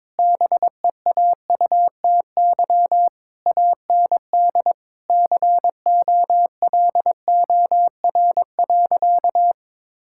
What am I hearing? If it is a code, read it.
Morse code, heard as a keyed tone, about 22 wpm, 700 Hz. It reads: BEAUTY AND COLOR.
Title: Sentences from Top 1600 Words 22wpm